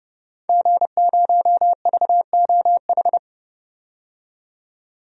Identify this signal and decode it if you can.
Morse code, heard as a keyed tone, about 30 wpm, 700 Hz. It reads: G0VOH
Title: DXCC-100 Call Signs - 5 Characters - 30wpm